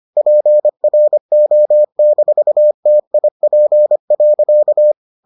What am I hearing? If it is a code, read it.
Morse code, heard as a keyed tone, about 25 wpm, 600 Hz. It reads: PRO-TIP.